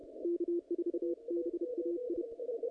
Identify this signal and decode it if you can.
Morse code, heard as a keyed tone, about 32 wpm, 340 Hz. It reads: K4BAI
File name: contests / naqp-cw-k5tr-op-k5pi- -0100